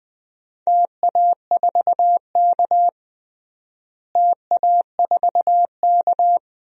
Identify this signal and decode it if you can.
Morse code, heard as a keyed tone, about 20 wpm, 700 Hz. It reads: TA4K TA4K
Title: DXCC-100 Call Signs - 4 Characters - Repeated 2x - 20wpm